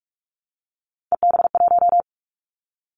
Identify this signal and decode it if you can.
Morse code, heard as a keyed tone, about 44 wpm, 700 Hz. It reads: EB1
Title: Callsign Prefixes for Top 25 DXCC - Repeated 3x - 45wpm